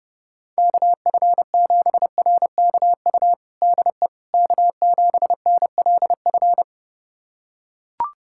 Morse code, heard as a keyed tone, about 32 wpm, 700 Hz. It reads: KF7RKU DE K7NLF
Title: USA Callsign DE Callsign 30wpm